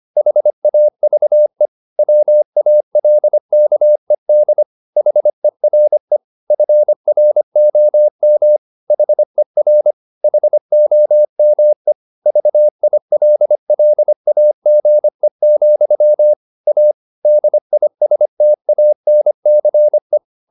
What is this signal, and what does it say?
Morse code, heard as a keyed tone, about 25 wpm, 600 Hz. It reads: HAVE WALKED HERE FROM HER HOME VILLAGE, A DISTANCE